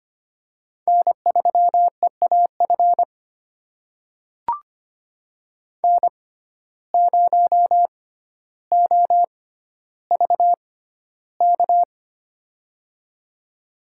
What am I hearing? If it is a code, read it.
Morse code, heard as a keyed tone, about 25 wpm, spaced as for 13 wpm, 700 Hz. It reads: N3EAF N0OVK
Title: USA Call Signs - 5 Characters - 6x Character Spacing - 25wpm